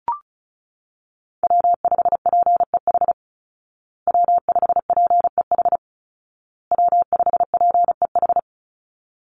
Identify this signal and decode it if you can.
Morse code, heard as a keyed tone, about 34 wpm, 700 Hz. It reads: W5PEH W5PEH W5PEH
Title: Call Signs Repeated 2x - 35wpm